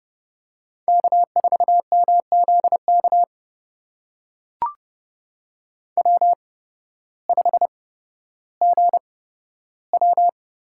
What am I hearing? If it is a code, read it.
Morse code, heard as a keyed tone, about 29 wpm, spaced as for 14 wpm, 700 Hz. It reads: K4MZK W5GW
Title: USA Call Signs - 5 Characters - 8x Character Spacing - 30wpm